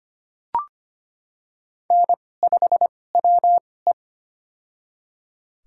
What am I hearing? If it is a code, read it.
Morse code, heard as a keyed tone, about 25 wpm, spaced as for 18 wpm, 700 Hz. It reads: N5WE
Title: USA Call Signs - 2x Character Spacing - 25wpm